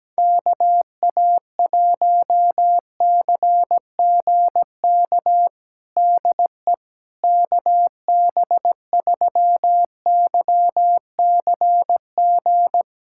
Morse code, heard as a keyed tone, about 17 wpm, 700 Hz. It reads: KA1CGK DE KB3YCG
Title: USA Callsign DE Callsign - 6 Characters - 17wpm